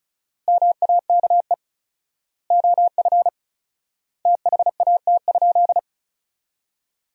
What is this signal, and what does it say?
Morse code, heard as a keyed tone, about 34 wpm, 700 Hz. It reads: MAKE OF THAT?
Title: 4x Word-Spacing - Sentences from Top 100 Words - 35wpm